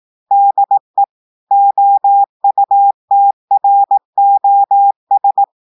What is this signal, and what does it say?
Morse code, heard as a keyed tone, about 18 wpm, 800 Hz. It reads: DE OUTROS
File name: audios / portugais / portugais texte en clair 1 vitesse 18 mots minutes